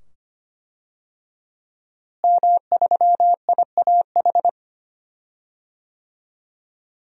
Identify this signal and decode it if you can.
Morse code, heard as a keyed tone, about 25 wpm, 700 Hz. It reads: M3IAH
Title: DXCC-100 Call Signs - 5 Characters - Repeated 2x - 25wpm